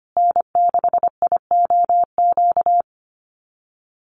Morse code, heard as a keyed tone, about 25 wpm, 700 Hz. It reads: N6IOQ
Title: USA Call Signs Repeated 3x 25wpm